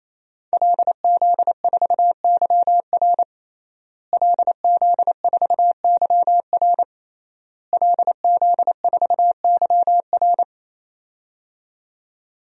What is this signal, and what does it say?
Morse code, heard as a keyed tone, about 28 wpm, 700 Hz. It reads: LZ4YR LZ4YR LZ4YR
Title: DXCC-100 Call Signs - 5 Characters - Repeated 3x - 28wpm